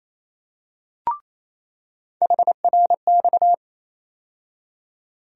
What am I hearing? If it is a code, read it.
Morse code, heard as a keyed tone, about 28 wpm, 700 Hz. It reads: HRX
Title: USA Call Sign Postfixes - 3 Letters 28wpm